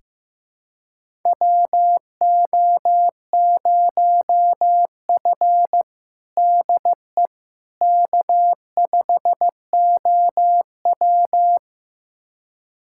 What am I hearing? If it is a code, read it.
Morse code, heard as a keyed tone, about 15 wpm, 700 Hz. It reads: WO0F DE K5OW
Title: USA Callsign DE Callsign - 4 Characters - 15wpm